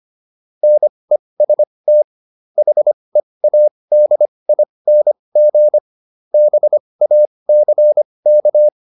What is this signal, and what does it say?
Morse code, heard as a keyed tone, about 25 wpm, spaced as for 20 wpm, 600 Hz. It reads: NEST HEADING BACK